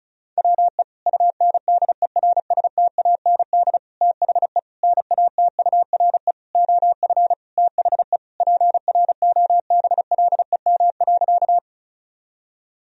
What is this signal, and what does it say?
Morse code, heard as a keyed tone, about 36 wpm, 700 Hz. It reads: WE UNDERSTAND THE NATURE OF THE PROBLEM.
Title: Sentences from Top 1000 Words - Review Part 2 - 35wpm